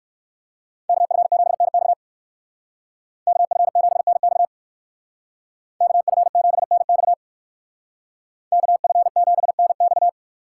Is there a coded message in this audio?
Yes: Morse code from 0.9 to 10.1 s, heard as a keyed tone, about 59 wpm, 700 Hz. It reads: KF7NX KF7NX KF7NX KF7NX